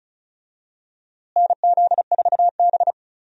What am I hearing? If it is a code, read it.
Morse code, heard as a keyed tone, about 34 wpm, 700 Hz. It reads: NZ4B
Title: USA Call Signs - 4 Characters - Repeated 3x - 35wpm